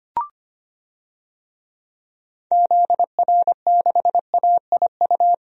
Morse code, heard as a keyed tone, about 25 wpm, 700 Hz. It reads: ZR6AIU
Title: DXCC-100 Call Signs - 6 Characters - 25wpm